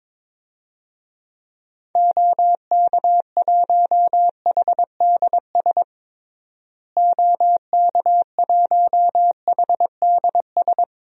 Morse code, heard as a keyed tone, about 22 wpm, 700 Hz. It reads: OK1HDS OK1HDS
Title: DXCC-100 Call Signs - 6 Characters - Repeated 3x - 22wpm